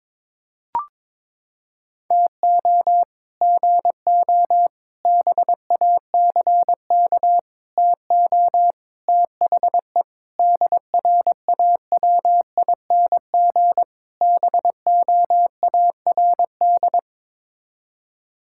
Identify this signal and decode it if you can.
Morse code, heard as a keyed tone, about 22 wpm, 700 Hz. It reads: TO GO BACK TO THE DRAWING BOARD
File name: practice / 22wpm / Sets of 7 Words - 22wpm